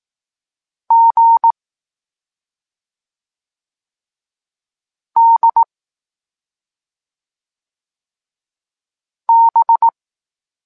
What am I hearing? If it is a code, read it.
Morse code, heard as a keyed tone, about 18 wpm, 920 Hz. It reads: G D B